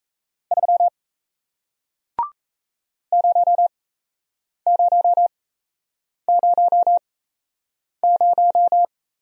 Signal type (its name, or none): Morse code